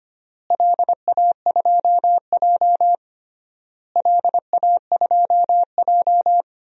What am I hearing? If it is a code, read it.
Morse code, heard as a keyed tone, about 25 wpm, 700 Hz. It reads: LA2J LA2J
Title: DXCC-100 Call Signs - 4 Characters - Repeated 3x - 25wpm